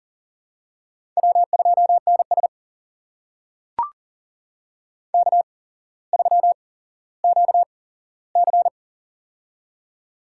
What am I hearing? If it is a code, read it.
Morse code, heard as a keyed tone, about 38 wpm, spaced as for 17 wpm, 700 Hz. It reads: W2NS K3QC
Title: USA Call Signs - 8x Character Spacing - 40wpm